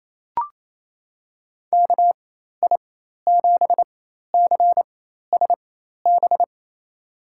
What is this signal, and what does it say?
Morse code, heard as a keyed tone, about 28 wpm, spaced as for 13 wpm, 700 Hz. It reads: KI7CSB
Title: USA Call Signs - 6 Characters - 4x Character Spacing - 28wpm